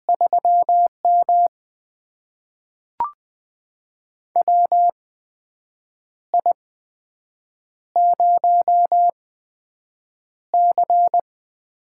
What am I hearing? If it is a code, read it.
Morse code, heard as a keyed tone, about 20 wpm, spaced as for 7 wpm, 700 Hz. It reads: 3M WI0C
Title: USA Call Signs - 4 Characters - 8x Character Spacing - 20wpm